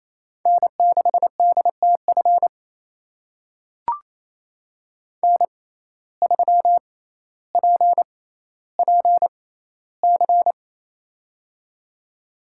Morse code, heard as a keyed tone, about 28 wpm, spaced as for 14 wpm, 700 Hz. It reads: N6DTF N3PPC
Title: USA Call Signs - 5 Characters - 6x Character Spacing - 28wpm